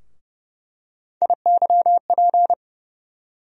Morse code, heard as a keyed tone, about 30 wpm, 700 Hz. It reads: IYP
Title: USA Call Sign Postfixes - 3 Letters 30wpm